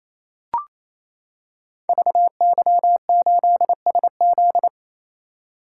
Morse code, heard as a keyed tone, about 27 wpm, 700 Hz. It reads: VY8SZ
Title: Code Groups - 5 Letter:Numbers - 28wpm